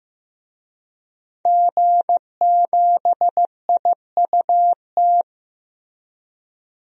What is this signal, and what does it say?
Morse code, heard as a keyed tone, about 16 wpm, 700 Hz. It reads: G7IUT
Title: DXCC-100 Call Signs - 5 Characters - Repeated 2x - 15wpm